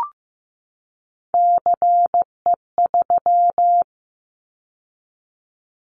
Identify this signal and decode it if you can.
Morse code, heard as a keyed tone, about 15 wpm, 700 Hz. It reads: CE3
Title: Callsign Prefixes for Top 100 DXCC - 15wpm